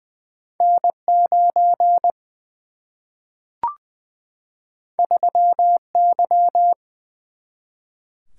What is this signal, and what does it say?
Morse code, heard as a keyed tone, about 20 wpm, 700 Hz. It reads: N9 3Y
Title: Single-Letter Single-Number 20wpm